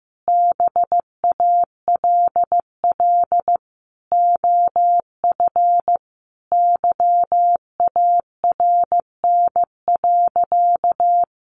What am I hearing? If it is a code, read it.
Morse code, heard as a keyed tone, about 15 wpm, 700 Hz. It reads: BALL OF YARN.
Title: Additional Easy Phrases 15wpm